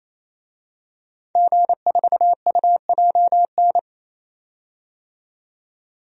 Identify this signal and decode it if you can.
Morse code, heard as a keyed tone, about 28 wpm, 700 Hz. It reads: G4UJN
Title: DXCC-100 Call Signs - 5 Characters - 28wpm